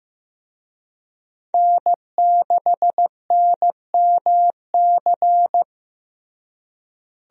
Morse code, heard as a keyed tone, about 15 wpm, 700 Hz. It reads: N6NMC
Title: USA Call Signs - 5 Characters - Repeated 2x - 15wpm